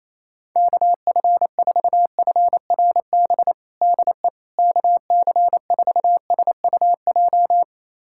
Morse code, heard as a keyed tone, about 28 wpm, 700 Hz. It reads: KF4FRB DE KC4SUJ
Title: USA Callsign DE Callsign - 6 Characters - 28wpm